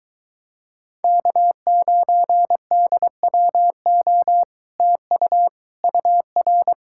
Morse code, heard as a keyed tone, about 22 wpm, 700 Hz. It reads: K9DWO TU UR